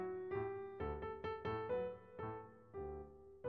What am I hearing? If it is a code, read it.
There is no code.